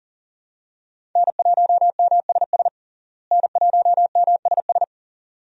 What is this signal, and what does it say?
Morse code, heard as a keyed tone, about 40 wpm, 700 Hz. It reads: N1MSS N1MSS
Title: USA Call Signs - 5 Characters - Repeated 2x - 40wpm